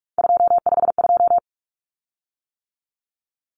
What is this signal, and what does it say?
Morse code, heard as a keyed tone, about 44 wpm, 700 Hz. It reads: 252
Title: Common Cut Numbers - 3-Digit 45wpm